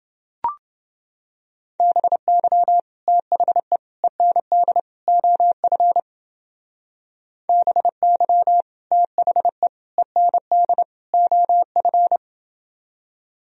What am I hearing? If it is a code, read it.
Morse code, heard as a keyed tone, about 28 wpm, 700 Hz. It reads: BY THE END OF BY THE END OF